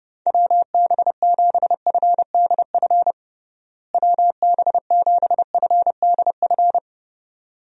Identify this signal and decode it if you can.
Morse code, heard as a keyed tone, about 32 wpm, 700 Hz. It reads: WB7FDF WB7FDF
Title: USA Call Signs - 6 Characters - Repeated 2x - 30wpm